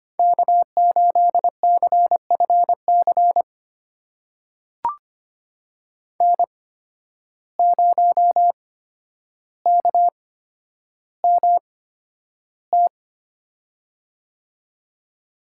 Morse code, heard as a keyed tone, about 25 wpm, spaced as for 11 wpm, 700 Hz. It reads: K8CFC N0KMT
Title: USA Call Signs - 5 Characters - 8x Character Spacing - 25wpm